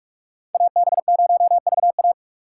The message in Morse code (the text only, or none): AB0VA